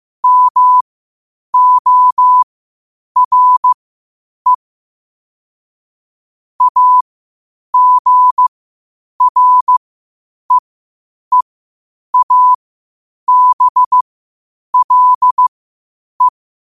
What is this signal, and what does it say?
Morse code, heard as a keyed tone, about 15 wpm, spaced as for 8 wpm, 1000 Hz. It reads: MORE AGREEABLE
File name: qotdfiles / 8wpm